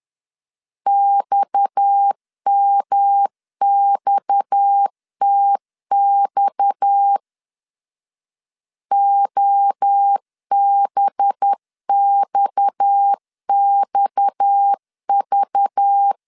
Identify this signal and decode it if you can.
Morse code, heard as a keyed tone, about 10 wpm, 800 Hz. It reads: XMXTX OBXXV
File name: LEZIONI / 6 PKX0 / 6 PKX 10wpm 2426